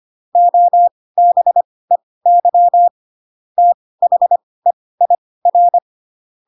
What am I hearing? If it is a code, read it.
Morse code, heard as a keyed tone, about 25 wpm, spaced as for 18 wpm, 700 Hz. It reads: OBEY THEIR
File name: Pinocchio / Pinocchio ch4 12 25 ik1hge